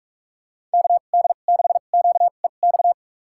Morse code, heard as a keyed tone, about 46 wpm, spaced as for 33 wpm, 700 Hz. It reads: KD6QEX
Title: USA Call Signs - 6 Characters - 2x Character Spacing - 45wpm